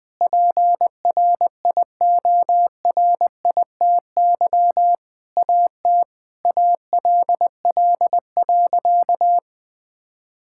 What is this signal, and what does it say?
Morse code, heard as a keyed tone, about 20 wpm, 700 Hz. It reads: PRIORITY AT ALL.
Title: Sentences from Top 1600 Words 20wpm